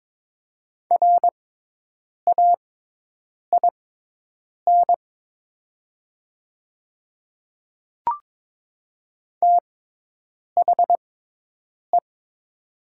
Morse code, heard as a keyed tone, about 22 wpm, spaced as for 8 wpm, 700 Hz. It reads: RAIN THE
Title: 6x Character Spacing - Top 300 Words - 22wpm